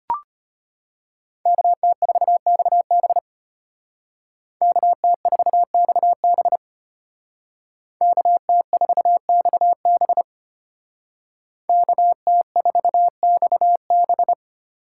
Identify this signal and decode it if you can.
Morse code, heard as a keyed tone, about 33 wpm, 700 Hz. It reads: KT4XB KT4XB KT4XB KT4XB